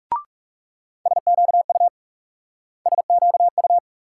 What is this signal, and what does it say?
Morse code, heard as a keyed tone, about 39 wpm, 700 Hz. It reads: SQU SQU